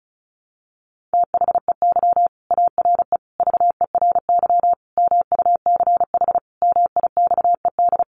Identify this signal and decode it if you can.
Morse code, heard as a keyed tone, about 34 wpm, 700 Hz. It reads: THEY ARE VERY MUCH MIXED
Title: Sentences from Top 1500 Words 35wpm